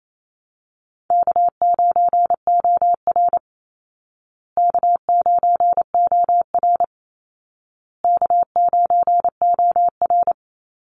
Morse code, heard as a keyed tone, about 28 wpm, 700 Hz. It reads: K9OR K9OR K9OR
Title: Easy Call Signs Repeated 3x - 28wpm